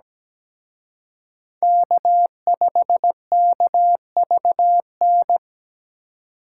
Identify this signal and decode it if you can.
Morse code, heard as a keyed tone, about 20 wpm, 700 Hz. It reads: K5KVN
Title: Speed Racing - USA Call Signs 17wpm